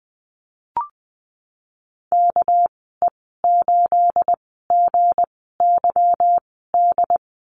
Morse code, heard as a keyed tone, about 20 wpm, spaced as for 15 wpm, 700 Hz. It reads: KE8GYD
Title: 245 Lesson 39g - Letter z - Call Signs 20wpm